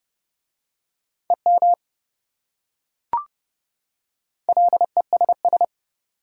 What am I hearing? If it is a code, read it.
Morse code, heard as a keyed tone, about 28 wpm, 700 Hz. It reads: EM LESS